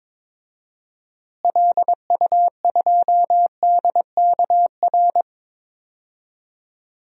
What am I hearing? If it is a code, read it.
Morse code, heard as a keyed tone, about 22 wpm, 700 Hz. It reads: LU2DKR
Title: DXCC-100 Call Signs - 6 Characters - Repeated 2x - 22wpm